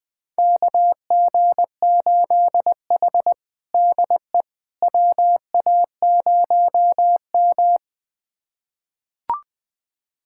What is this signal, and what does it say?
Morse code, heard as a keyed tone, about 20 wpm, 700 Hz. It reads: KG8H DE WA0M
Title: USA Callsign DE Callsign - 4 Characters - 20wpm